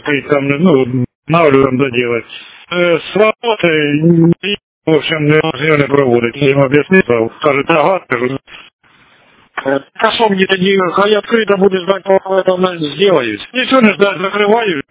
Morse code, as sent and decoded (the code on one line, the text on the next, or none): none